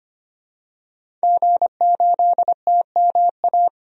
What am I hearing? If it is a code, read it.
Morse code, heard as a keyed tone, about 25 wpm, 700 Hz. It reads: G8TMA
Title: DXCC-100 Call Signs - 5 Characters - Repeated 2x - 25wpm